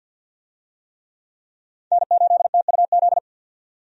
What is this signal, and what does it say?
Morse code, heard as a keyed tone, about 52 wpm, 700 Hz. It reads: N8TUZ
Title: USA Call Signs - 5 Characters - 50wpm